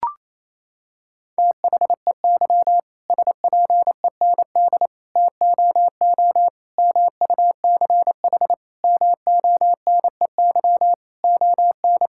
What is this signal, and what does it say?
Morse code, heard as a keyed tone, about 28 wpm, 700 Hz. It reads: THEY SPEND TOO MUCH MONEY ON
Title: Sentences from Top 1000 Words - Review Part 5 - 28wpm